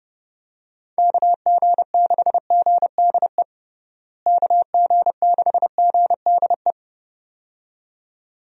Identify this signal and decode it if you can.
Morse code, heard as a keyed tone, about 30 wpm, 700 Hz. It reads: KG6GDE KG6GDE
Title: USA Call Signs - 6 Characters - Repeated 2x - 30wpm